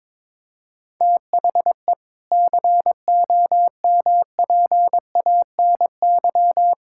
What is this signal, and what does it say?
Morse code, heard as a keyed tone, about 22 wpm, 700 Hz. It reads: THE COMPANY
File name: practice / 22wpm / Sentences with 8-Word Phrases - 22wpm